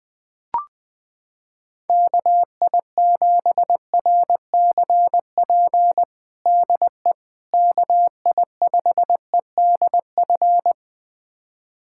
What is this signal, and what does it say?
Morse code, heard as a keyed tone, about 20 wpm, 700 Hz. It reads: KI7RCP DE KI5EDF
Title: USA Callsign DE Callsign 20wpm